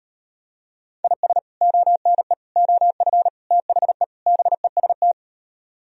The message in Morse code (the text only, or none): IS ONE OF THE BEST